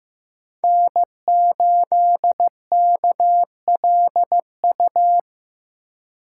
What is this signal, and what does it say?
Morse code, heard as a keyed tone, about 16 wpm, 700 Hz. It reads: N8KLU
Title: USA Call Signs - 5 Characters - 3x Character Spacing - 15wpm